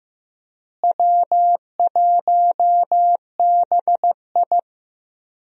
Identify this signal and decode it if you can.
Morse code, heard as a keyed tone, about 15 wpm, 700 Hz. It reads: W1BI
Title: USA Call Signs - 4 Characters - Repeated 3x - 15wpm